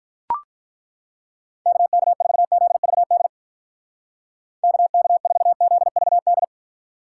Morse code, heard as a keyed tone, about 43 wpm, 700 Hz. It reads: KK4ZVD KK4ZVD
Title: Speed Racing - USA 2x3 Call Signs 35wpm